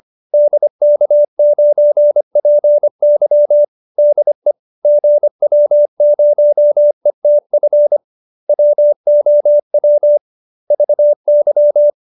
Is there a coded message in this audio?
Yes: Morse code from 0.3 to 11.9 s, heard as a keyed tone, about 25 wpm, 590 Hz. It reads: DK9PY DE GW0ETF WOW VY